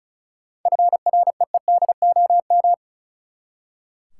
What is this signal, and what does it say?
Morse code, heard as a keyed tone, about 34 wpm, 700 Hz. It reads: FREEDOM
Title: Seven Letter Words - 35wpm